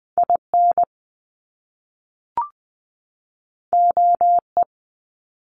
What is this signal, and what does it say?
Morse code, heard as a keyed tone, about 20 wpm, 700 Hz. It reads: I N O E
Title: Top Two Letter Combinations in Words 20wpm